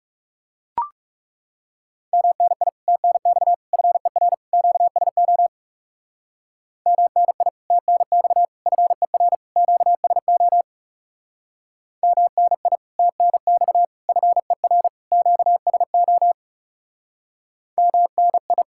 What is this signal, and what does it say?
Morse code, heard as a keyed tone, about 37 wpm, 700 Hz. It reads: MNI TNX FER QSO MNI TNX FER QSO MNI TNX FER QSO MNI
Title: Speed-Racing Common QSO Phrases - 30wpm